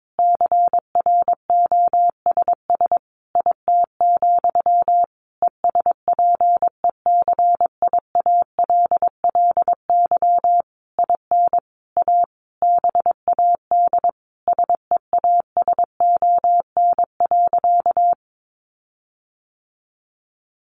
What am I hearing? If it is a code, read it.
Morse code, heard as a keyed tone, about 22 wpm, 700 Hz. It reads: CROSS IT, ESPECIALLY IN A BAD SEASON.
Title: Sentences from Top 1400 Words 22wpm